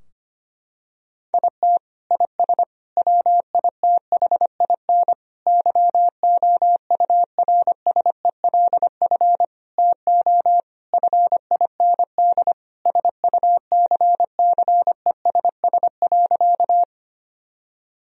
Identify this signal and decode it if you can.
Morse code, heard as a keyed tone, about 25 wpm, 700 Hz. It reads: IT IS WITHIN YOURSELF TO FIND SUCCESS.
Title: Sentences from Top 1000 Words - Review Part 4 - 25wpm